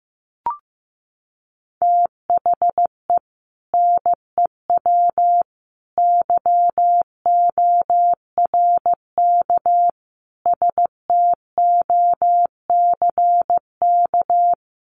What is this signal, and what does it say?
Morse code, heard as a keyed tone, about 15 wpm, 700 Hz. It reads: THE NEW YORK STOCK
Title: Sets of 5 Words 15wpm